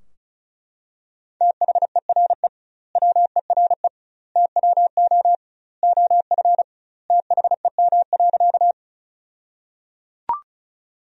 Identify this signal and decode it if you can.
Morse code, heard as a keyed tone, about 36 wpm, 700 Hz. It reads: THERE WERE TWO OF THEM.
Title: 2x Word-Spacing - Sentences from Top 100 Words - 35wpm